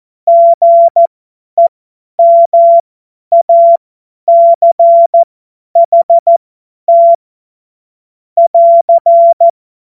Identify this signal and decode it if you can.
Morse code, heard as a keyed tone, about 14 wpm, spaced as for 10 wpm, 680 Hz. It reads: GEMACHT <AR>
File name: Morsekurs / Witze / Witz 28. September 2021, 50 F70